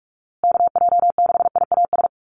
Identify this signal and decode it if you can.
Morse code, heard as a keyed tone, about 44 wpm, 700 Hz. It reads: KJ6IAS
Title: USA Call Signs Repeated 2x 45wpm